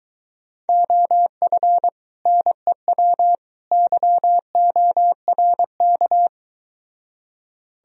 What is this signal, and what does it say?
Morse code, heard as a keyed tone, about 24 wpm, 700 Hz. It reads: OF NEW YORK